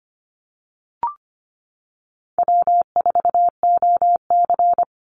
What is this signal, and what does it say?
Morse code, heard as a keyed tone, about 25 wpm, 700 Hz. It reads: W4OC
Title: Easy Call Signs Repeated 3x - 25wpm